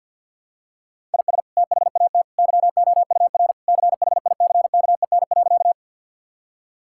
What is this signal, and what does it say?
Morse code, heard as a keyed tone, about 50 wpm, 700 Hz. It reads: IS THAT YOUR CHICKEN.
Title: Sentences from Top 1000 Words - Review Part 5 - 50wpm